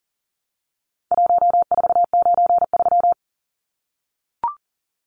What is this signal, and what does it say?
Morse code, heard as a keyed tone, about 40 wpm, 700 Hz. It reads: 1493
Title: Numbers - Contest Style - 4-Digit 40wpm